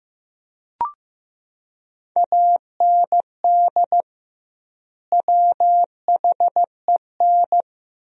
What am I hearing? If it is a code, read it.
Morse code, heard as a keyed tone, about 16 wpm, 700 Hz. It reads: AND WHEN
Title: 2x Word-Spacing - Sentences from Top 100 Words - 15wpm